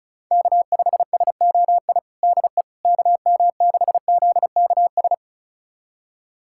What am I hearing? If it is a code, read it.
Morse code, heard as a keyed tone, about 36 wpm, 700 Hz. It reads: K5SOI DE KM6ZKS
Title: USA Callsign DE Callsign 35wpm